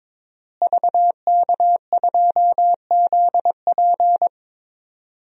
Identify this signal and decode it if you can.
Morse code, heard as a keyed tone, about 22 wpm, 700 Hz. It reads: VK2ZP
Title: DXCC-100 Call Signs - 5 Characters - Repeated 2x - 22wpm